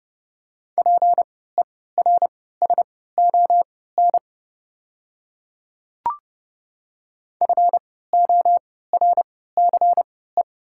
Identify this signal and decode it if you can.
Morse code, heard as a keyed tone, about 32 wpm, spaced as for 17 wpm, 700 Hz. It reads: PERSON FORCE
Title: 3x Character Spacing - Top 500 Words - 30wpm